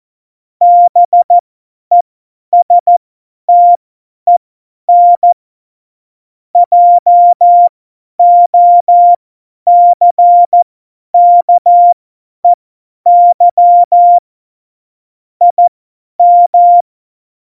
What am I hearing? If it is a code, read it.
Morse code, heard as a keyed tone, about 14 wpm, spaced as for 10 wpm, 700 Hz. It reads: BESTEN JOCKEY IM
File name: Morsekurs / Witze / Witz 18 August 2020, 50 F70